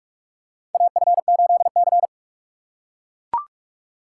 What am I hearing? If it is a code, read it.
Morse code, heard as a keyed tone, about 41 wpm, 700 Hz. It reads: AF8C
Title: USA Call Signs - 3x Character Spacing - 45wpm